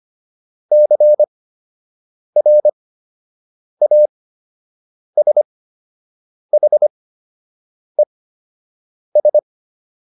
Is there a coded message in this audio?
Yes: Morse code from 0.7 to 9.4 s, heard as a keyed tone, about 25 wpm, spaced as for 7 wpm, 600 Hz. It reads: CRASHES